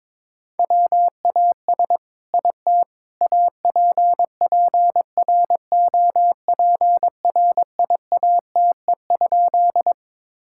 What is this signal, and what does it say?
Morse code, heard as a keyed tone, about 22 wpm, 700 Hz. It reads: WAS IT APPROPRIATE?
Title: Sentences from Top 1000 Words - Review Part 5 - 22wpm